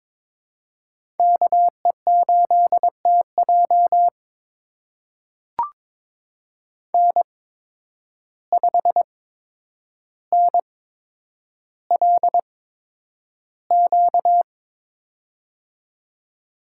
Morse code, heard as a keyed tone, about 22 wpm, spaced as for 9 wpm, 700 Hz. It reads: KE8TJ N5NLQ
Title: USA Call Signs - 5 Characters - 8x Character Spacing - 22wpm